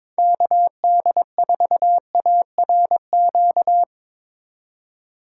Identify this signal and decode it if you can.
Morse code, heard as a keyed tone, about 21 wpm, 700 Hz. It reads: KD4ARQ
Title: USA Call Signs - 6 Characters - 4x Character Spacing - 22wpm